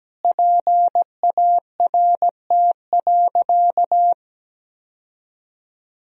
Morse code, heard as a keyed tone, about 17 wpm, 700 Hz. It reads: PART.
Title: Sentences from Top 1000 Words - Review Part 1 - 17wpm